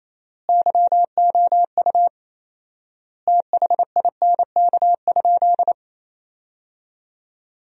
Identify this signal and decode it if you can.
Morse code, heard as a keyed tone, about 28 wpm, 700 Hz. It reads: YOU THINK?
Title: 4x Word-Spacing - Sentences from Top 200 Words - 28wpm